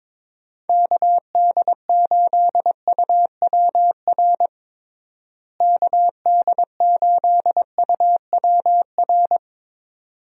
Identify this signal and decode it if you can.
Morse code, heard as a keyed tone, about 22 wpm, 700 Hz. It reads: KD8UWR KD8UWR
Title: USA Call Signs - 6 Characters - Repeated 2x - 22wpm